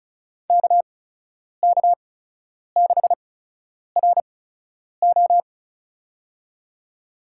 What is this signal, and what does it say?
Morse code, heard as a keyed tone, about 36 wpm, spaced as for 10 wpm, 700 Hz. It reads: KK6RO
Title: USA Call Signs - 5 Characters - 8x Character Spacing - 35wpm